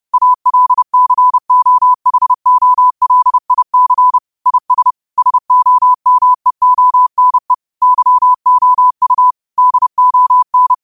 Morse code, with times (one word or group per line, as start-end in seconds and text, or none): 0.1-4.2 ALCOHOLIC
4.5-4.9 IS
5.2-7.5 SOMEONE
7.8-9.3 YOU
9.6-10.7 DON